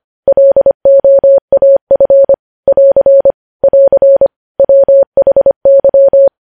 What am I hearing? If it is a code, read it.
Morse code, heard as a keyed tone, about 25 wpm, 550 Hz. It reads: LOAF <AR> <AR> WHY
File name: podcast / News headlines-1x-25x25